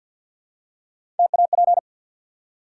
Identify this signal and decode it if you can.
Morse code, heard as a keyed tone, about 54 wpm, 700 Hz. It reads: TAP
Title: Three Letter Words - 50wpm